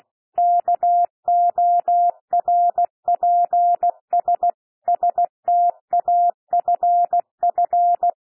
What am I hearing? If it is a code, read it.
Morse code, heard as a keyed tone, about 16 wpm, 700 Hz. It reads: KORPS STAFF
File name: cwmp3 / P1 SOUND PASSAGE 16 wpm